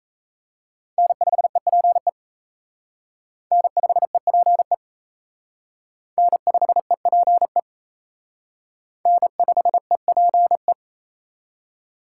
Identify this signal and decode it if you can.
Morse code, heard as a keyed tone, about 35 wpm, 700 Hz. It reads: N5EPE N5EPE N5EPE N5EPE
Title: Speed Racing - USA 1x3 Call Signs 28wpm